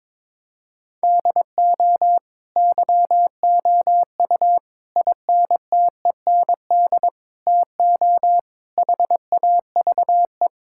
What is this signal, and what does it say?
Morse code, heard as a keyed tone, about 22 wpm, 700 Hz. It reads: DO YOU INTEND TO HAVE
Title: Sentences from Top 1600 Words 22wpm